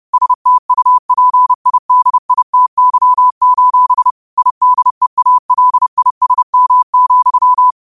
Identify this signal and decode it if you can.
Morse code, heard as a keyed tone, about 30 wpm, 1000 Hz. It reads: STUPIDITY: IDEALISM,
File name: qotdfiles / 30wpm